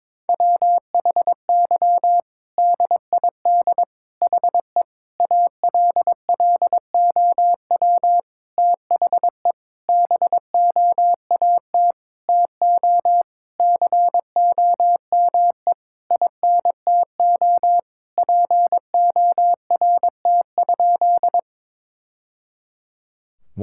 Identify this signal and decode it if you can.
Morse code, heard as a keyed tone, about 22 wpm, 700 Hz. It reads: WHY DID HE ALLOW THE BOAT TO COME INTO PORT?
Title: Sentences from Top 1000 Words - Review Part 5 - 22wpm